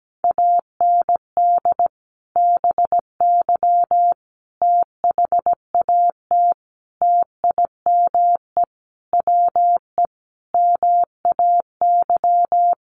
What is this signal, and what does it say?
Morse code, heard as a keyed tone, about 17 wpm, 700 Hz. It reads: AND BY THAT TIME WE MAY
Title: Sentences from Top 1100 Words 17wpm